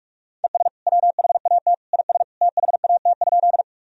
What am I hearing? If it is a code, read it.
Morse code, heard as a keyed tone, about 43 wpm, 700 Hz. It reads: ES WHAT IS THAT?
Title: Sentences from Top 100 Words and Abbreviations 45wpm